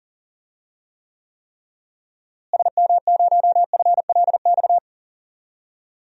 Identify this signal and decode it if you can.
Morse code, heard as a keyed tone, about 37 wpm, 700 Hz. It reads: SM0FLX
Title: DXCC-100 Call Signs - 6 Characters - Repeated 3x - 40wpm